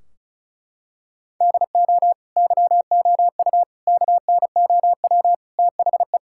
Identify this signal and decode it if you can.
Morse code, heard as a keyed tone, about 36 wpm, 700 Hz. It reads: DO YOU KNOW THE
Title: Sentences from Top 1600 Words 35wpm